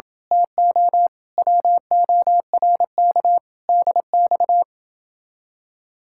Morse code, heard as a keyed tone, about 23 wpm, 700 Hz. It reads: TO WORK DX
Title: Speed-Racing Common QSO Phrases - 20wpm